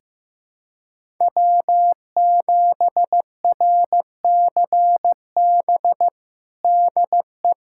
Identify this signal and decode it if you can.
Morse code, heard as a keyed tone, about 15 wpm, 700 Hz. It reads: W7RCB DE
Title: USA Callsign DE Callsign - 5 Characters - 15wpm